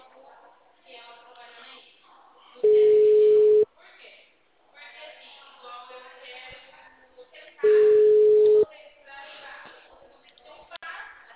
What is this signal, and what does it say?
Ringback tone (Europe, 425 Hz), with speech.
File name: out- -2053- - - .61325